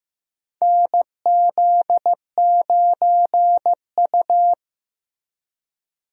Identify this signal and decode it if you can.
Morse code, heard as a keyed tone, about 15 wpm, 700 Hz. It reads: NZ9U
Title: USA Call Signs - 4 Characters - 15wpm